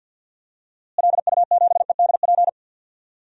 Morse code, heard as a keyed tone, about 53 wpm, 700 Hz. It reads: LU7EDP
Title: DXCC-100 Call Signs - 6 Characters - Repeated 2x - 50wpm